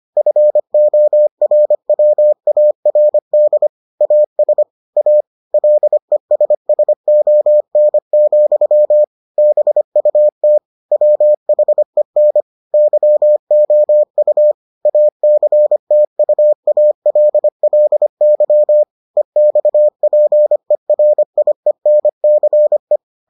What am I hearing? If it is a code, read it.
Morse code, heard as a keyed tone, about 25 wpm, 600 Hz. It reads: FORWARD AS A LESSON, BUT WHEN YOU ACTUALLY EXPERIENCE